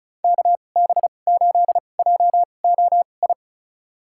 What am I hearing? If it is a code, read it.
Morse code, heard as a keyed tone, about 36 wpm, spaced as for 25 wpm, 700 Hz. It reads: KB8JOI
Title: USA Call Signs - 6 Characters - 2x Character Spacing - 35wpm